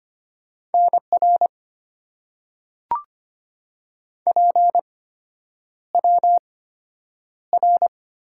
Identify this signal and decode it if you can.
Morse code, heard as a keyed tone, about 25 wpm, spaced as for 9 wpm, 700 Hz. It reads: NR PWR
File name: Top 90 QSO Elements - 8x Character Spacing - 25wpm